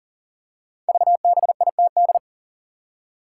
Morse code, heard as a keyed tone, about 38 wpm, 700 Hz. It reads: VBITD
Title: Code Groups - 5 Letters - 40wpm